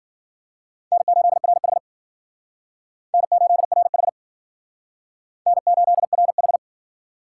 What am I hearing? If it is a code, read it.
Morse code, heard as a keyed tone, about 50 wpm, 700 Hz. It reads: N8RH N8RH N8RH
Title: Speed Racing - USA Call Signs 40wpm